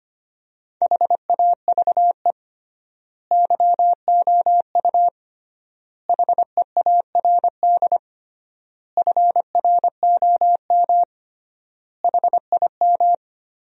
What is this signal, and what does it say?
Morse code, heard as a keyed tone, about 25 wpm, 700 Hz. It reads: HAVE YOU HEARD FROM HIM
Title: 3x Word-Spacing - Sentences from Top 400 Words - 25wpm